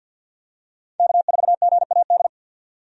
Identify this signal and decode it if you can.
Morse code, heard as a keyed tone, about 51 wpm, 700 Hz. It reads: K4GAD
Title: USA Call Signs - 5 Characters - 50wpm